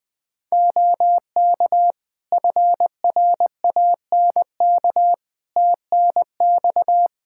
Morse code, heard as a keyed tone, about 21 wpm, 700 Hz. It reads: OK FRANK TNX
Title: Practice QSO - Exchange 4 - Ending - 20wpm